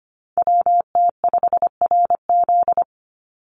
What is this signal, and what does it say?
Morse code, heard as a keyed tone, about 25 wpm, 700 Hz. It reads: WT5RZ
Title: Call Signs Repeated 3x - 25wpm